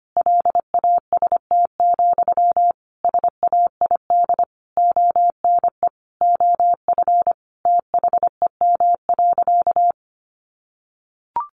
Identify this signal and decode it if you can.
Morse code, heard as a keyed tone, about 25 wpm, 700 Hz. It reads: LAST, SAID ONE OF THEM.
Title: Sentences from Top 300 Words 25wpm